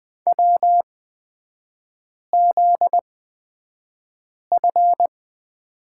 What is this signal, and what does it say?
Morse code, heard as a keyed tone, about 20 wpm, spaced as for 5 wpm, 700 Hz. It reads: WZF
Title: 243 Lesson 39e - Letter z - Three Characters 20wpm